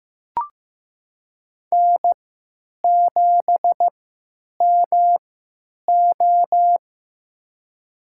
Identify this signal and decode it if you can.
Morse code, heard as a keyed tone, about 15 wpm, spaced as for 9 wpm, 700 Hz. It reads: N7MO
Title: USA Call Signs - 3x Character Spacing - 15wpm